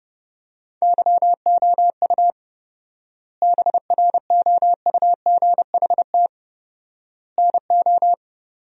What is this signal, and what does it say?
Morse code, heard as a keyed tone, about 32 wpm, 700 Hz. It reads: YOU BROUGHT NO
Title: 4x Word-Spacing - Sentences from Top 500 Words - 30wpm